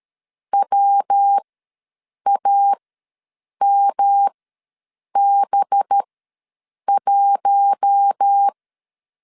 Morse code, heard as a keyed tone, about 13 wpm, spaced as for 7 wpm, 800 Hz. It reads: WAMB1
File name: LEZIONI / 6 PKX0 / 6 PK ws 12wpm 3116